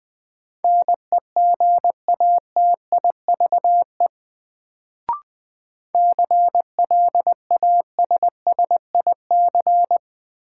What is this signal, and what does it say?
Morse code, heard as a keyed tone, about 20 wpm, 700 Hz. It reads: NEGATIVE CLASSIC
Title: Sentences from Top 1000 Words - New Word Drill - 20wpm